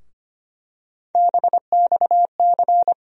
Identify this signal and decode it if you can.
Morse code, heard as a keyed tone, about 25 wpm, 700 Hz. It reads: BXC
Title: USA Call Sign Postfixes - 3 Letters 25wpm